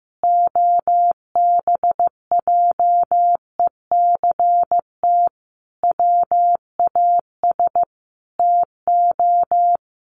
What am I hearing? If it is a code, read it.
Morse code, heard as a keyed tone, about 15 wpm, 700 Hz. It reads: OBJECT WAS TO
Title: Sentences from Top 1500 Words 15wpm